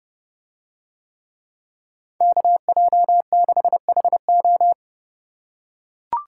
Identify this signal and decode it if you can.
Morse code, heard as a keyed tone, about 30 wpm, 700 Hz. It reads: KJ6HO